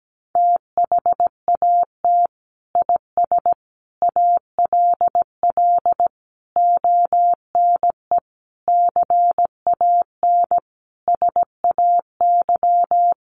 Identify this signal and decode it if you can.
Morse code, heard as a keyed tone, about 17 wpm, 700 Hz. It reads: THAT IS ALL ONE CAN SAY
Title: Sentences from Top 200 Words 17wpm